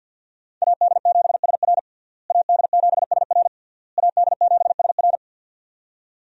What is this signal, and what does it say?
Morse code, heard as a keyed tone, about 52 wpm, 700 Hz. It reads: AD7SR AD7SR AD7SR
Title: USA Call Signs - 5 Characters - Repeated 3x - 50wpm